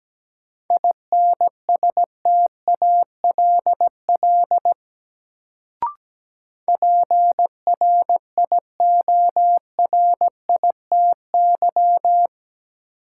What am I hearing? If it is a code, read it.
Morse code, heard as a keyed tone, about 17 wpm, 700 Hz. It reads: INSTALL PRIORITY
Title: Sentences from Top 1600 Words - New Word Drill - 17wpm